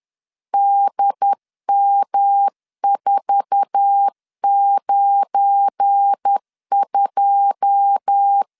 Morse code, heard as a keyed tone, about 10 wpm, 800 Hz. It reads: DM492
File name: LEZIONI / 10 SIGNS / 10A SLASH 10wpm 2808